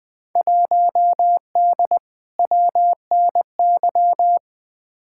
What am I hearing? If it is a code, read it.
Morse code, heard as a keyed tone, about 20 wpm, 700 Hz. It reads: 1D WNY
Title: Field Day Exchanges - 20wpm